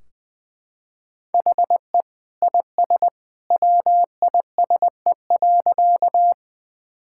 Morse code, heard as a keyed tone, about 20 wpm, 700 Hz. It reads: HE IS WISE.
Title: Sentences from Top 1600 Words 20wpm